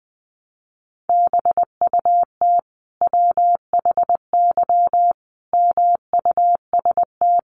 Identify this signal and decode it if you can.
Morse code, heard as a keyed tone, about 20 wpm, 700 Hz. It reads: BUT WHY MUST
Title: Sentences from Top 400 Words 20wpm